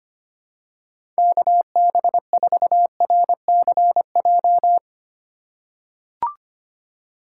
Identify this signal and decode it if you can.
Morse code, heard as a keyed tone, about 25 wpm, 700 Hz. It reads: KB4RCJ
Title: USA Call Signs - 6 Characters - 2x Character Spacing - 25wpm